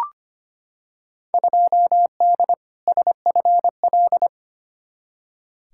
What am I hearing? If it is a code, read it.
Morse code, heard as a keyed tone, about 25 wpm, 700 Hz. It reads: 2D SFL
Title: Field Day Exchanges Encore - 25wpm